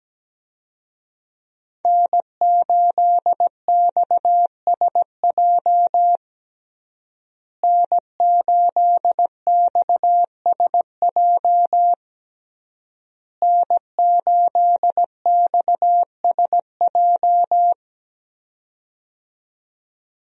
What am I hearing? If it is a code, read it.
Morse code, heard as a keyed tone, about 17 wpm, 700 Hz. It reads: N8XSJ N8XSJ N8XSJ
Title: USA Call Signs - 5 Characters - Repeated 3x - 17wpm